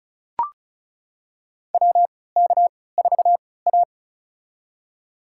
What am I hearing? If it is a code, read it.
Morse code, heard as a keyed tone, about 36 wpm, spaced as for 20 wpm, 700 Hz. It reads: WK4A
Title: USA Call Signs - 4 Characters - 3x Character Spacing - 35wpm